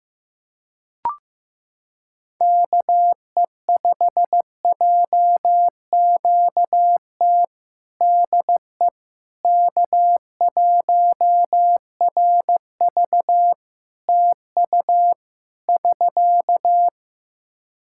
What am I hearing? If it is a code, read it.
Morse code, heard as a keyed tone, about 15 wpm, 700 Hz. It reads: KE5JQT DE K1RV TU <SK>
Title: Practice QSO - Exchange 4 - Ending - 15wpm